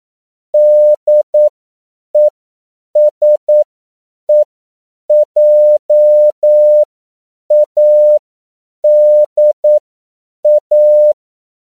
Morse code, heard as a keyed tone, about 9 wpm, spaced as for 7 wpm, 600 Hz. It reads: DESEJADA
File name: CW / 10PPM